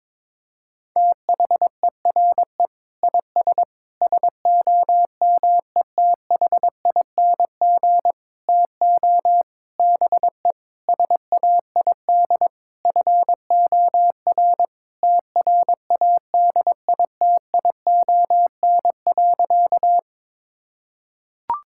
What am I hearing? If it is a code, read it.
Morse code, heard as a keyed tone, about 22 wpm, 700 Hz. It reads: THERE IS SOMETHING TO BE SAID FOR TRADITION.
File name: Sentences with 7-Word Phrases - 22wpm